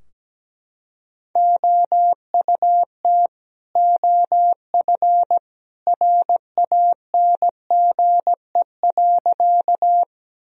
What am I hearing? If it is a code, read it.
Morse code, heard as a keyed tone, about 17 wpm, 700 Hz. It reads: OUT OF RANGE.
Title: Sentences from Top 1000 Words - Review Part 4 - 17wpm